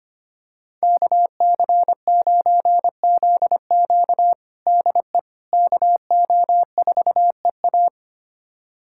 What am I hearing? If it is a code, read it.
Morse code, heard as a keyed tone, about 25 wpm, 700 Hz. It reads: KC9ZQ DE KO4EA
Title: USA Callsign DE Callsign - 5 Characters - 25wpm